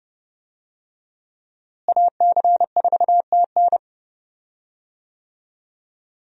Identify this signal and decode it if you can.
Morse code, heard as a keyed tone, about 32 wpm, 700 Hz. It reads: AC4TN
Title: USA Call Signs - 5 Characters - Repeated 3x - 30wpm